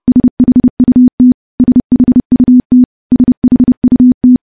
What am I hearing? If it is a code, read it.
Morse code, heard as a keyed tone, about 30 wpm, 250 Hz. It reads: SHUT SHUT SHUT
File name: morse / morse-training-30WPM